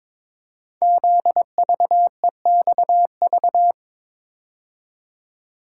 Morse code, heard as a keyed tone, about 22 wpm, 700 Hz. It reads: ZVEXV
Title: Code Groups - 5 Letters - 22wpm